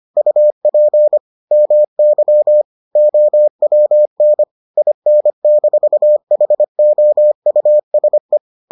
Morse code, heard as a keyed tone, about 25 wpm, 600 Hz. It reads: UP MY OWN IN-HOUSE